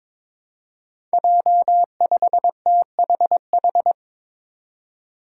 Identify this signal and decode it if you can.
Morse code, heard as a keyed tone, about 22 wpm, 700 Hz. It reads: J5THH
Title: Five Challenging Letter-Number Combinations 22wpm